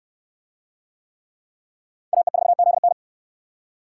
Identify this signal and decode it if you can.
Morse code, heard as a keyed tone, about 65 wpm, 690 Hz. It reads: AE4ZN